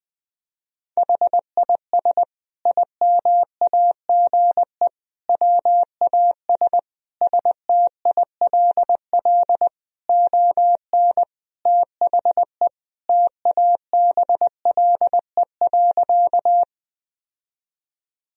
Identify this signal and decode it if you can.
Morse code, heard as a keyed tone, about 21 wpm, 700 Hz. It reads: HIS IMAGE WAS STILL ON THE TABLE.
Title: Sentences from Top 1000 Words - Review Part 4 - 20wpm